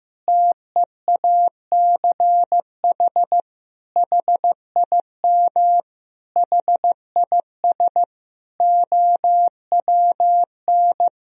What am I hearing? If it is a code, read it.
Morse code, heard as a keyed tone, about 15 wpm, 700 Hz. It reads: TEACH HIM HIS OWN
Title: Sentences from Top 1000 Words - Review Part 2 - 15wpm